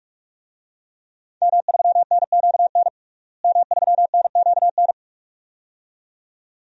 Morse code, heard as a keyed tone, about 41 wpm, 700 Hz. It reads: M3NQN M3NQN
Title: DXCC-100 Call Signs - 5 Characters - Repeated 2x - 45wpm